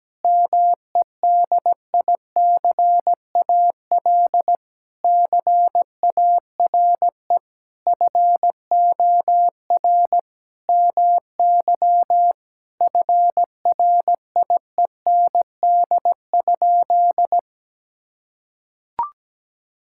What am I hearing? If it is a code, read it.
Morse code, heard as a keyed tone, about 17 wpm, 700 Hz. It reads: MEDICAL CARE FOR MY FRIEND?
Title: Sentences from Top 1000 Words - Review Part 5 - 17wpm